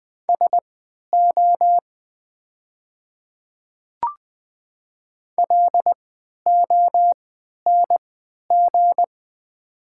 Morse code, heard as a keyed tone, about 20 wpm, spaced as for 11 wpm, 700 Hz. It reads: SO LONG
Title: 3x Character Spacing - Top 100 Words - 20wpm